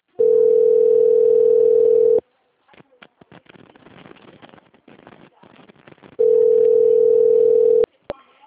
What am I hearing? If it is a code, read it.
Ringback tone (North America), 440 + 480 Hz.